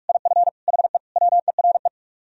Morse code, heard as a keyed tone, about 48 wpm, 700 Hz. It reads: IF HE WERE